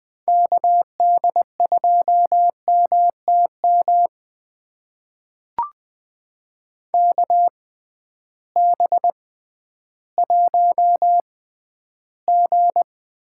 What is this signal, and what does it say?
Morse code, heard as a keyed tone, about 20 wpm, spaced as for 12 wpm, 700 Hz. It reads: KD2MTM KB1G
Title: USA Call Signs - 6 Characters - 6x Character Spacing - 20wpm